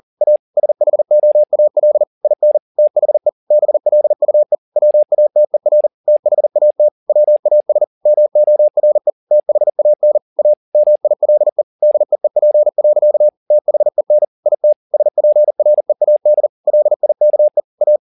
Morse code, heard as a keyed tone, about 40 wpm, 600 Hz. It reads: A SHOAL IN THE BLUE WATER THAT WAS MORE THAN A MILE DEEP. THEN IT SPREAD LIKE A
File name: the old man and the sea 2 40wpm